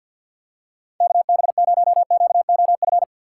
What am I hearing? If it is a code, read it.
Morse code, heard as a keyed tone, about 52 wpm, 700 Hz. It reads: KB0QOF
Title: USA Call Signs - 6 Characters - 4x Character Spacing - 50wpm